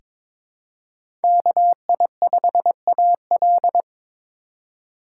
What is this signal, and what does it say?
Morse code, heard as a keyed tone, about 22 wpm, 700 Hz. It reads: KI5AL
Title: K1USN SST QSO Practice - 22wpm